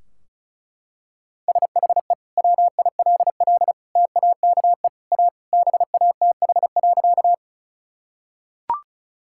Morse code, heard as a keyed tone, about 34 wpm, 700 Hz. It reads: SHE WILL TAKE A BATH.
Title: Sentences from Top 1600 Words 35wpm